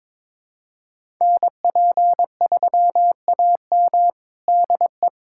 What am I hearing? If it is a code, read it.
Morse code, heard as a keyed tone, about 22 wpm, 700 Hz. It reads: NP3AM DE
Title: USA Callsign DE Callsign 22wpm